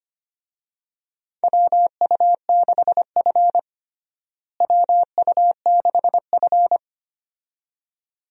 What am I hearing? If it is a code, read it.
Morse code, heard as a keyed tone, about 25 wpm, 700 Hz. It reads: WU6F WU6F
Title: USA Call Signs - 4 Characters - Repeated 2x - 25wpm